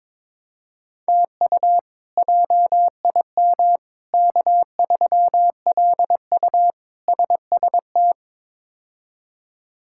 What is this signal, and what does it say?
Morse code, heard as a keyed tone, about 22 wpm, 700 Hz. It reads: TU JIM K3LU SST
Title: K1USN SST QSO Practice - 22wpm